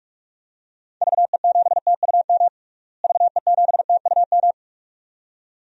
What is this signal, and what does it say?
Morse code, heard as a keyed tone, about 46 wpm, 700 Hz. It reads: VE7TUM VE7TUM
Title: DXCC-100 Call Signs - 6 Characters - Repeated 2x - 45wpm